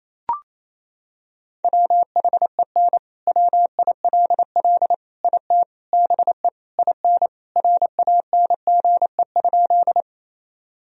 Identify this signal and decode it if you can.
Morse code, heard as a keyed tone, about 27 wpm, 700 Hz. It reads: WHEN WILL IT BE IN RANGE?
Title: Sentences from Top 1000 Words - Review Part 4 - 28wpm